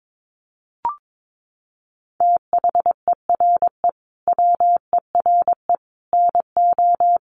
Morse code, heard as a keyed tone, about 22 wpm, 700 Hz. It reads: THERE WERE NO
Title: Sets of 5 Words Encore2 22wpm